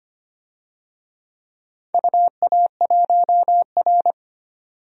Morse code, heard as a keyed tone, about 25 wpm, 700 Hz. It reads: UA1R
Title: DXCC-100 Call Signs - 4 Characters - 25wpm